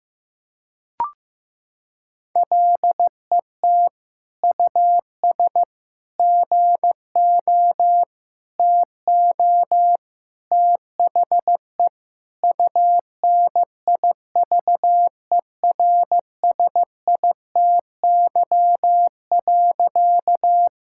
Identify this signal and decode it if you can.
Morse code, heard as a keyed tone, about 15 wpm, 700 Hz. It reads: LET US GO TO THE UNIVERSITY.
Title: Sentences from Top 1000 Words - Review Part 5 - 15wpm